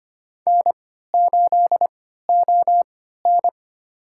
Morse code, heard as a keyed tone, about 25 wpm, spaced as for 14 wpm, 700 Hz. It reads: N8ON
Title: USA Call Signs - 4 Characters - 3x Character Spacing - 25wpm